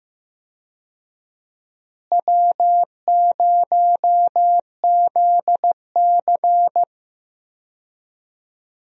Morse code, heard as a keyed tone, about 15 wpm, 700 Hz. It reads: W0ZC
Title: USA Call Signs - 4 Characters - Repeated 3x - 15wpm